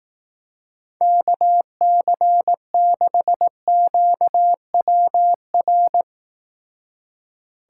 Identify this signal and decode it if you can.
Morse code, heard as a keyed tone, about 18 wpm, 700 Hz. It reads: KC6QWR
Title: Speed Racing - USA 2x3 Call Signs 15wpm